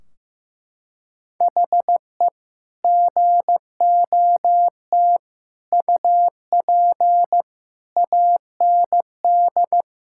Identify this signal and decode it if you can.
Morse code, heard as a keyed tone, about 15 wpm, 700 Hz. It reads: HE GOT UP AND
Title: Sentences from Top 1600 Words 15wpm